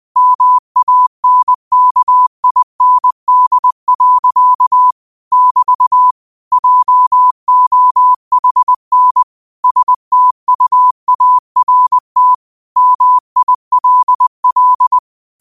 Morse code, heard as a keyed tone, about 20 wpm, 1000 Hz. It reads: MANKIND. <BT> JOHN STUART MILL